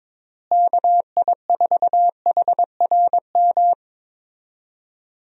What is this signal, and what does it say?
Morse code, heard as a keyed tone, about 22 wpm, 700 Hz. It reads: KI4HRM
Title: USA Call Signs - 6 Characters - Repeated 3x - 22wpm